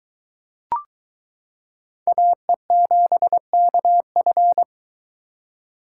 Morse code, heard as a keyed tone, about 22 wpm, 700 Hz. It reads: AE7KF